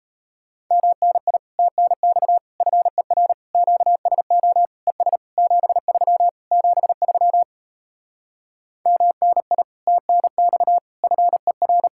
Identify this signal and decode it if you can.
Morse code, heard as a keyed tone, about 37 wpm, 700 Hz. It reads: MNI TNX FER QSO ES 73 73 MNI TNX FER